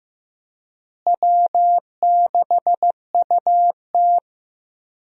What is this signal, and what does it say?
Morse code, heard as a keyed tone, about 15 wpm, 700 Hz. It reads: W6UT
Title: USA Call Signs - 4 Characters - Repeated 2x - 15wpm